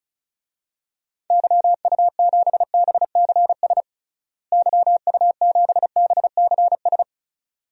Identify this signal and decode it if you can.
Morse code, heard as a keyed tone, about 36 wpm, 700 Hz. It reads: YU7BCS YU7BCS
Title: DXCC-100 Call Signs - 6 Characters - Repeated 2x - 35wpm